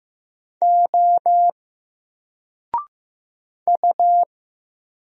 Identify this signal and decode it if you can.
Morse code, heard as a keyed tone, about 15 wpm, 700 Hz. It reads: O U
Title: Single Letter-Number Alt 15wpm